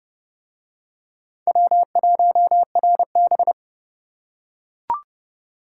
Morse code, heard as a keyed tone, about 28 wpm, 700 Hz. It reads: W1RB